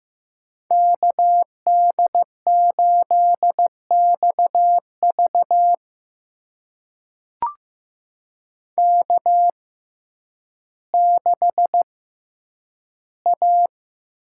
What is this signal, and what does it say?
Morse code, heard as a keyed tone, about 15 wpm, spaced as for 9 wpm, 700 Hz. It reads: KD8XV K6A
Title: USA Call Signs - 6x Character Spacing - 15wpm